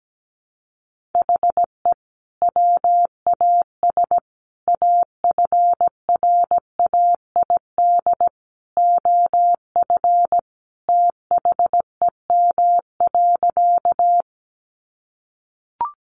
Morse code, heard as a keyed tone, about 17 wpm, 700 Hz. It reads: HE WAS AFRAID OF THEM.
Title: Sentences from Top 1500 Words 17wpm